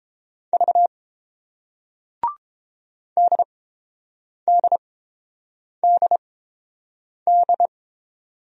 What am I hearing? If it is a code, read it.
Morse code, heard as a keyed tone, about 30 wpm, spaced as for 7 wpm, 700 Hz. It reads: V DDDD